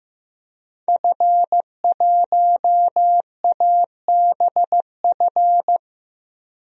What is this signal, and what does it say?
Morse code, heard as a keyed tone, about 15 wpm, 700 Hz. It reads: F1ABF
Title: DXCC-100 Call Signs - 5 Characters - Repeated 3x - 15wpm